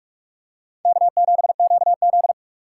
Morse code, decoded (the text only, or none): K7QZ